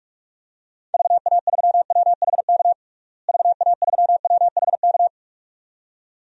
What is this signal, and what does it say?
Morse code, heard as a keyed tone, about 46 wpm, 700 Hz. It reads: VA3WHK VA3WHK
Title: DXCC-100 Call Signs - 6 Characters - Repeated 2x - 45wpm